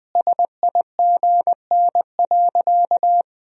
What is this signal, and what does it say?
Morse code, heard as a keyed tone, about 20 wpm, 700 Hz. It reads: SIGN.
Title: Sentences from Top 1000 Words - Review Part 2 - 20wpm